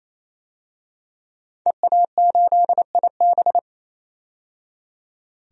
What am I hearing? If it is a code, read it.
Morse code, heard as a keyed tone, about 28 wpm, 700 Hz. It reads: EA8IB
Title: DXCC-100 Call Signs - 5 Characters - 28wpm